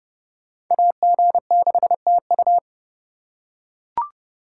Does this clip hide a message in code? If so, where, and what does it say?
Yes: Morse code from 0.7 to 2.6 s, heard as a keyed tone, about 30 wpm, 700 Hz. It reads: AG6TU